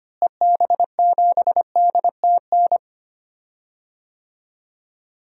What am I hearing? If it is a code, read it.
Morse code, heard as a keyed tone, about 25 wpm, 700 Hz. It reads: EB7DTN
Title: DXCC-100 Call Signs - 6 Characters - Repeated 2x - 25wpm